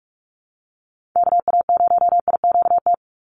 Morse code, heard as a keyed tone, about 44 wpm, 700 Hz. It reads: KA0IQT
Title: Call Signs - 45wpm